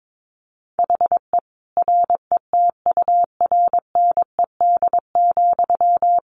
Morse code, heard as a keyed tone, about 22 wpm, 700 Hz. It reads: HE RETURNED,
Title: A Sound Of Thunder 22wpm